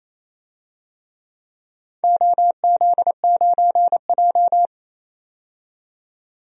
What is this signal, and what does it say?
Morse code, heard as a keyed tone, about 28 wpm, 700 Hz. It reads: OZ9J
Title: DXCC-100 Call Signs - 4 Characters - 28wpm